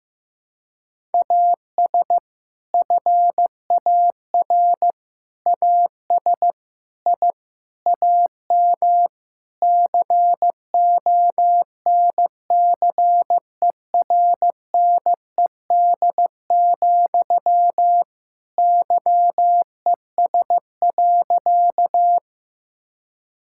Morse code, heard as a keyed tone, about 15 wpm, 700 Hz. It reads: AS FAR AS I AM CONCERNED, YES.
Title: Sentences from Top 1000 Words - Review Part 6 - 15wpm